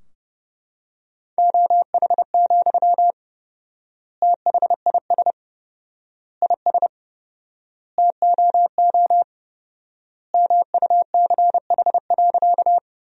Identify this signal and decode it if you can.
Morse code, heard as a keyed tone, about 32 wpm, 700 Hz. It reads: OH, THIS IS TOO MUCH.
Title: 4x Word-Spacing - Sentences from Top 500 Words - 30wpm